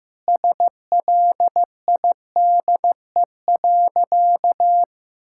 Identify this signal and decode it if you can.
Morse code, heard as a keyed tone, about 15 wpm, 700 Hz. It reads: SLIDE.
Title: Sentences from Top 1600 Words 15wpm